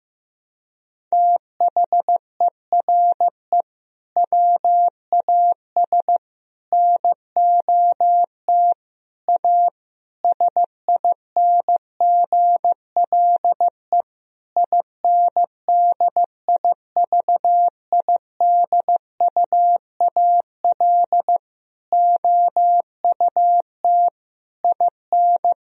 Morse code, heard as a keyed tone, about 15 wpm, 700 Hz. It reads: THERE WAS NOT A SINGLE INDIVIDUAL OUT IN
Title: Sentences from Top 1000 Words - Review Part 4 - 15wpm